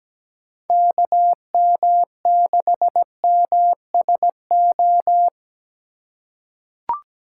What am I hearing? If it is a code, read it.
Morse code, heard as a keyed tone, about 17 wpm, 700 Hz. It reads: KM6MSO
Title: USA Call Signs - 6 Characters - 2x Character Spacing - 17wpm